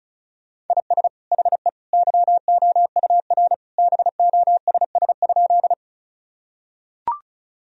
Morse code, heard as a keyed tone, about 34 wpm, 700 Hz. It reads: IS HE YOUR BOSS?
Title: Sentences from Top 1000 Words - Review Part 4 - 35wpm